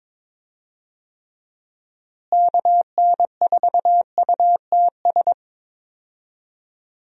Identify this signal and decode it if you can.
Morse code, heard as a keyed tone, about 22 wpm, 700 Hz. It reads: KN4UTS
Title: USA Call Signs - 6 Characters - Repeated 3x - 22wpm